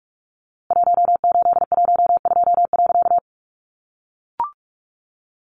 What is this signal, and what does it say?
Morse code, heard as a keyed tone, about 44 wpm, 700 Hz. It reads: 1812.
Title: A Sound Of Thunder 45wpm